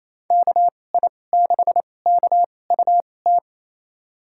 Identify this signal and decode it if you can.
Morse code, heard as a keyed tone, about 28 wpm, spaced as for 20 wpm, 700 Hz. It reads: KI6KUT
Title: USA Call Signs - 6 Characters - 2x Character Spacing - 28wpm